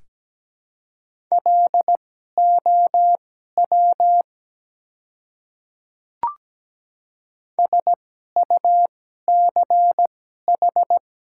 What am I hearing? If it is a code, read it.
Morse code, heard as a keyed tone, about 17 wpm, spaced as for 12 wpm, 700 Hz. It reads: LOW SUCH
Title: 2x Character Spacing - Top 300 Words - 17wpm